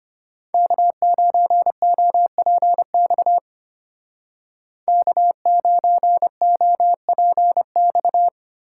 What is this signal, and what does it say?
Morse code, heard as a keyed tone, about 32 wpm, 700 Hz. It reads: K9OPX K9OPX